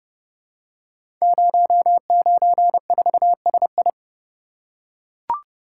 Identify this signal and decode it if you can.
Morse code, heard as a keyed tone, about 30 wpm, 700 Hz. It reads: 094SI
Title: Code Groups - 5 Letter:Numbers - 30wpm